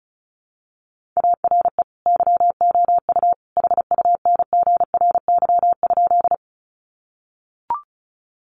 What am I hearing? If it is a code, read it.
Morse code, heard as a keyed tone, about 34 wpm, 700 Hz. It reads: ARE YOU HUNGRY?
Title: Sentences from Top 1400 Words 35wpm